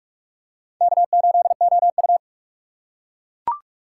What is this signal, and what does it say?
Morse code, heard as a keyed tone, about 44 wpm, 690 Hz. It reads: K8OU
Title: USA Call Signs - 4 Characters - 3x Character Spacing - 45wpm